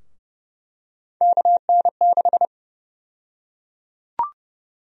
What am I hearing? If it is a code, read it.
Morse code, heard as a keyed tone, about 30 wpm, 700 Hz. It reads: KN6
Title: USA Call Sign Prefixes - 30wpm